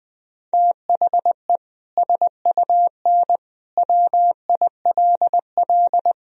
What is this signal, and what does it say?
Morse code, heard as a keyed tone, about 21 wpm, 700 Hz. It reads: THE SUN WILL
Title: Sentences from Top 1000 Words - Review Part 3 - 20wpm